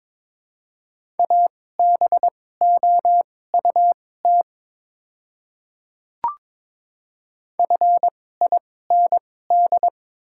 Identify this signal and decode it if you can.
Morse code, heard as a keyed tone, about 22 wpm, spaced as for 16 wpm, 700 Hz. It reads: ABOUT FIND
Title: 2x Character Spacing - Top 100 Words - 22wpm